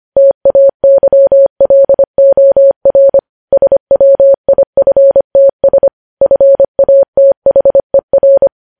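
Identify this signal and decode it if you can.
Morse code, heard as a keyed tone, about 25 wpm, 550 Hz. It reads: TAYLOR SWIFTS FATHER
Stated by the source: Morse code news headlines at 25 wpm.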